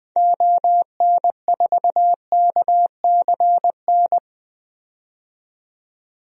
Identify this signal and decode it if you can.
Morse code, heard as a keyed tone, about 20 wpm, 700 Hz. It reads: ON4KCN
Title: DXCC-100 Call Signs - 6 Characters - 20wpm